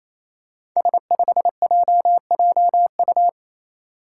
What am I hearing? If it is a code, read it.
Morse code, heard as a keyed tone, about 29 wpm, 700 Hz. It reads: S5JJU
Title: Five Challenging Letter-Number Combinations 28wpm